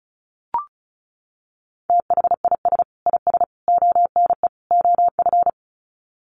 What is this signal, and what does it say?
Morse code, heard as a keyed tone, about 34 wpm, 700 Hz. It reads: THIS IS ONE OF
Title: Sets of 4 Words 35wpm